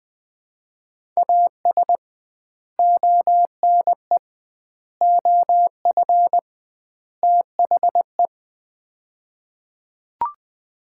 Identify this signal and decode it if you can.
Morse code, heard as a keyed tone, about 20 wpm, 700 Hz. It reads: AS ONE OF THE
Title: 2x Word-Spacing - Sets of 4 Words - 20wpm